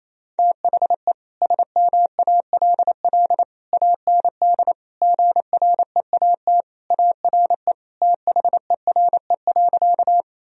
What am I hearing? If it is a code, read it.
Morse code, heard as a keyed tone, about 28 wpm, 700 Hz. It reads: THE SMALL AND GREAT ARE THERE.
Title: Sentences from Top 1000 Words - Review Part 1 - 28wpm